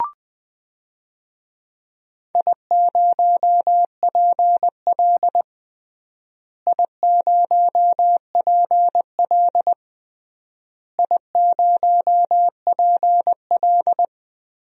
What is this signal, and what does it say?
Morse code, heard as a keyed tone, about 21 wpm, 700 Hz. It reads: I0PL I0PL I0PL
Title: DXCC-100 Call Signs - 4 Characters - Repeated 3x - 20wpm